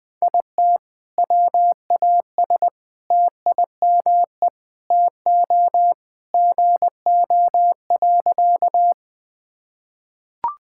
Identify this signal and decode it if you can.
Morse code, heard as a keyed tone, about 20 wpm, 700 Hz. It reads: IT WAS TIME TO GO.
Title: Sentences with 4-Word Phrases - 20wpm